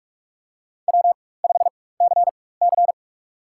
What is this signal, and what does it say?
Morse code, heard as a keyed tone, about 46 wpm, spaced as for 21 wpm, 700 Hz. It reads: W5CC
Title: USA Call Signs - 4 Characters - 4x Character Spacing - 45wpm